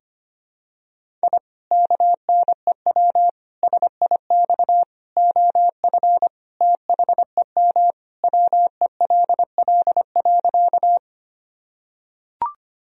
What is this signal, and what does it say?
Morse code, heard as a keyed tone, about 25 wpm, 700 Hz. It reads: I KNEW SIX OF THEM WELL.
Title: Sentences from Top 1000 Words - Review Part 2 - 25wpm